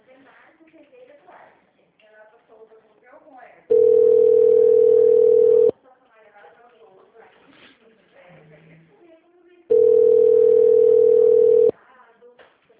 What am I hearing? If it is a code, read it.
Ringback tone (North America), 440 + 480 Hz, with speech.
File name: out- -2020- - - .47646